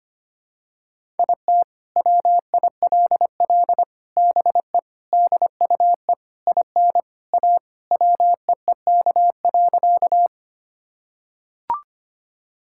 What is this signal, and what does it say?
Morse code, heard as a keyed tone, about 25 wpm, 700 Hz. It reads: IT WILL BE DUE IN A WEEK.
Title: Sentences from Top 1000 Words - Review Part 6 - 25wpm